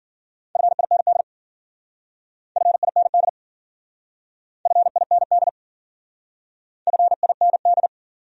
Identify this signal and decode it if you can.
Morse code, heard as a keyed tone, about 51 wpm, 700 Hz. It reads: FIND FIND FIND FIND